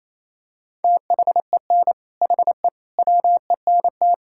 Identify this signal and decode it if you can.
Morse code, heard as a keyed tone, about 28 wpm, 700 Hz. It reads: THEN HE WENT
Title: Sentences from Top 1000 Words - Review Part 5 - 28wpm